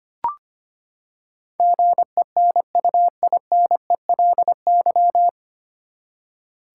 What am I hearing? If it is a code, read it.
Morse code, heard as a keyed tone, about 25 wpm, 700 Hz. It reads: GENUINELY
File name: Nine Letter Words - 25wpm